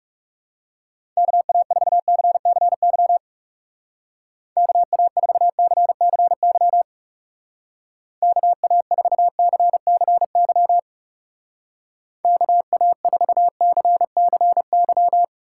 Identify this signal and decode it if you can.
Morse code, heard as a keyed tone, about 40 wpm, 700 Hz. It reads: KA4CCY KA4CCY KA4CCY KA4CCY